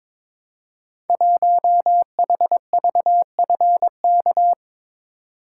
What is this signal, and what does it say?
Morse code, heard as a keyed tone, about 22 wpm, 700 Hz. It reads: 1HVFK
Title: Code Groups - 5 Letter:Numbers - 22wpm